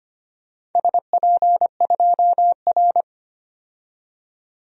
Morse code, heard as a keyed tone, about 25 wpm, 700 Hz. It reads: SP2R
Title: DXCC-100 Call Signs - 4 Characters - 25wpm